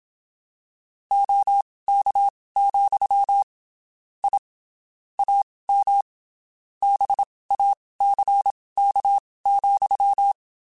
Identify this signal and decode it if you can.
Morse code, heard as a keyed tone, about 27 wpm, spaced as for 19 wpm, 800 Hz. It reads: OK, I AM BACK,
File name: (spaced) / QSOs Spaced-mp3s / WA7QOX 2 @25 sp